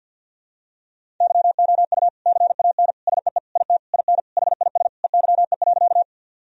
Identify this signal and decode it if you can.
Morse code, heard as a keyed tone, about 51 wpm, 700 Hz. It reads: YOU CAN SEE IT IN HIS EYE.
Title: Sentences from Top 1000 Words - Review Part 1 - 50wpm